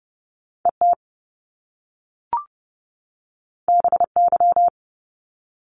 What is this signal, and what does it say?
Morse code, heard as a keyed tone, about 30 wpm, 700 Hz. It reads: ET BY
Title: Top Two Letter Combinations in Words 30wpm